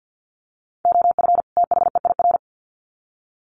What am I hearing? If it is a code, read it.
Morse code, heard as a keyed tone, about 50 wpm, 700 Hz. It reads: OF THEIR
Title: Sets of 2 Words 50wpm